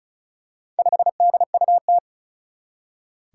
Morse code, heard as a keyed tone, about 35 wpm, 700 Hz. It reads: 5DUT
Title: Four Challenging Letter-Number Combinations 35wpm